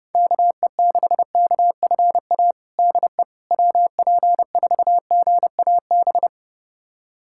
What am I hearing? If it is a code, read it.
Morse code, heard as a keyed tone, about 32 wpm, 700 Hz. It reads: KE6KFA DE WP4GAB
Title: USA Callsign DE Callsign - 6 Characters - 30wpm